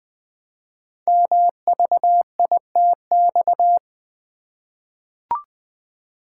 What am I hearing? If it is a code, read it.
Morse code, heard as a keyed tone, about 20 wpm, 700 Hz. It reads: MVITX
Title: Code Groups - 5 Letters - 20wpm